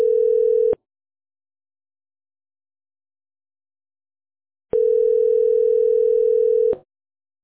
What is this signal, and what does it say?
Ringback tone (North America), 440 + 480 Hz.